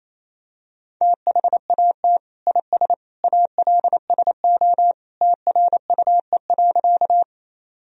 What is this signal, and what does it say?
Morse code, heard as a keyed tone, about 28 wpm, 700 Hz. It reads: THAT IS ALSO TRUE.
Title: Sentences from Top 1000 Words - Review Part 1 - 28wpm